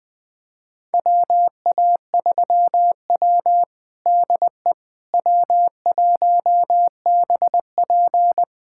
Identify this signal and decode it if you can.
Morse code, heard as a keyed tone, about 20 wpm, 700 Hz. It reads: WA3W DE W1BP
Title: USA Callsign DE Callsign - 4 Characters - 20wpm